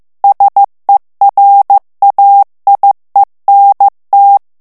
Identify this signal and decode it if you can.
Morse code, heard as a keyed tone, about 15 wpm, 800 Hz. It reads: SERAIENT
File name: morse mp3 / FRANCAIS TEXTE CLAIR 2 VITESSE 14 MOTS MINUTES